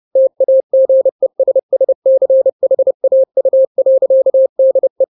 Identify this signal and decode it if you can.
Morse code, heard as a keyed tone, about 30 wpm, 530 Hz. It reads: TAGESSCHAU.DE
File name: 29wpm